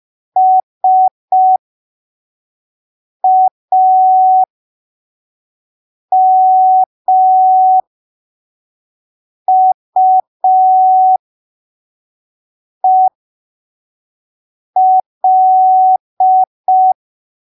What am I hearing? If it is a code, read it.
Morse code, heard as a keyed tone, about 5 wpm, spaced as for 3 wpm, 750 Hz. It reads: SAMUEL
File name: arquivos / texto / texto 17 05ppm